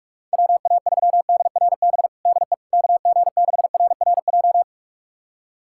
Morse code, heard as a keyed tone, about 46 wpm, 700 Hz. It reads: WA3DRB DE KG6RRJ
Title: USA Callsign DE Callsign - 6 Characters - 45wpm